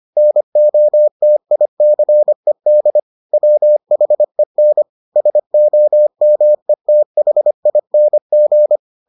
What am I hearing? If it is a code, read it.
Morse code, heard as a keyed tone, about 25 wpm, 600 Hz. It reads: NOTICED WHEN SOMETHING